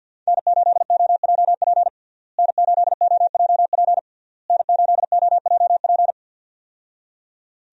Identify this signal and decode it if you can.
Morse code, heard as a keyed tone, about 48 wpm, 700 Hz. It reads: N8OJP N8OJP N8OJP
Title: USA Call Signs - 5 Characters - Repeated 3x - 50wpm